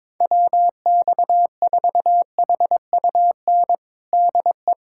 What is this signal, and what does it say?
Morse code, heard as a keyed tone, about 22 wpm, 700 Hz. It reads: WX4HUN DE
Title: USA Callsign DE Callsign - 6 Characters - 22wpm